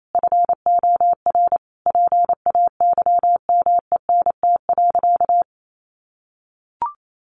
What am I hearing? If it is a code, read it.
Morse code, heard as a keyed tone, about 28 wpm, 700 Hz. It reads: FOR PAYMENT.
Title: Sentences from Top 1200 Words 28wpm